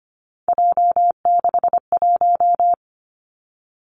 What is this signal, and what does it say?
Morse code, heard as a keyed tone, about 25 wpm, 700 Hz. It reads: J61
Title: Three Challenging Letter-Number Combinations 25wpm